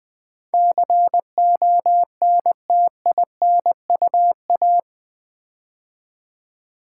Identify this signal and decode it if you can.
Morse code, heard as a keyed tone, about 20 wpm, 700 Hz. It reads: CONTINUA
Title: Build A Word - 4 Letters At A Time - 11-Letter Words - 20wpm